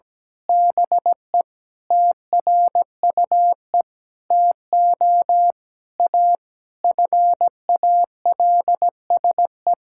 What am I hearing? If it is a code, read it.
Morse code, heard as a keyed tone, about 17 wpm, 700 Hz. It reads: BE TRUE TO A FALSE
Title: Sentences from Top 1600 Words 17wpm